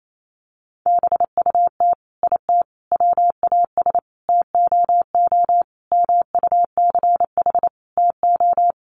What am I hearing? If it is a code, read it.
Morse code, heard as a keyed tone, about 28 wpm, 700 Hz. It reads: BUT IT WAS TOO MUCH TO
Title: Sentences from Top 1100 Words 28wpm